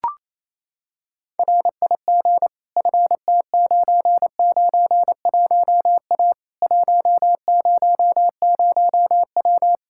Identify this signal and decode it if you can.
Morse code, heard as a keyed tone, about 27 wpm, 700 Hz. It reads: RIG FT991A 100W
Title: K1USN SST QSO Practice 28wpm